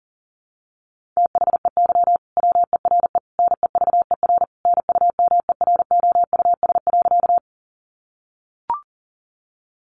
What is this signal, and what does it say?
Morse code, heard as a keyed tone, about 40 wpm, 700 Hz. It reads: THEY WERE NEVER NUMEROUS.
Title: Sentences from Top 1400 Words 40wpm